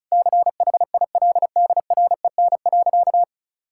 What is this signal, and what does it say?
Morse code, heard as a keyed tone, about 34 wpm, 700 Hz. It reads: CHILDREN.
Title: Sentences from Top 1600 Words 35wpm